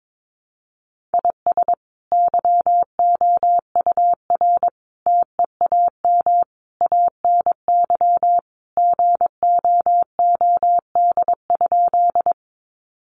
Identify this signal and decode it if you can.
Morse code, heard as a keyed tone, about 22 wpm, 700 Hz. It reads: IS YOUR TEAM ANY GOOD?
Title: Sentences from Top 1500 Words 22wpm